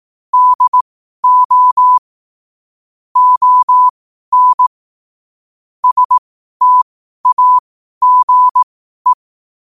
Morse code, heard as a keyed tone, about 18 wpm, spaced as for 12 wpm, 1000 Hz. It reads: DO ON STAGE
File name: qotdfiles / 12wpm